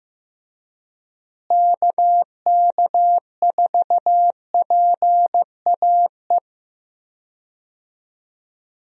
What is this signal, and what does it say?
Morse code, heard as a keyed tone, about 15 wpm, 700 Hz. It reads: KK4PAE
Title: USA Call Signs - 6 Characters - Repeated 3x - 15wpm